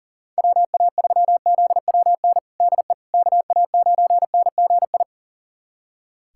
Morse code, heard as a keyed tone, about 37 wpm, 700 Hz. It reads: WA3ZWN DE KA9NGI
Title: USA Callsign DE Callsign 40wpm